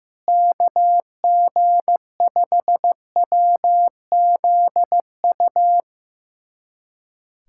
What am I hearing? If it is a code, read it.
Morse code, heard as a keyed tone, about 15 wpm, 700 Hz. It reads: KG5WZU
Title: USA Callsign DE Callsign - 6 Characters - 15wpm